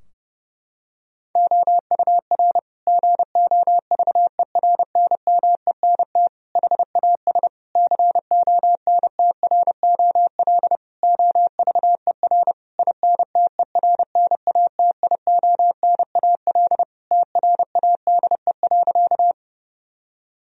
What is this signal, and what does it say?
Morse code, heard as a keyed tone, about 30 wpm, 700 Hz. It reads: OUR GOVERNMENT HAS CONTROL OVER INTERNATIONAL TRADE.
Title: Sentences from Top 1000 Words - Review Part 4 - 30wpm